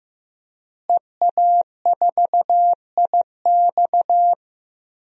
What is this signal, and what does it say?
Morse code, heard as a keyed tone, about 15 wpm, 700 Hz. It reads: EA4IX
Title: DXCC-100 Call Signs - 5 Characters - Repeated 3x - 15wpm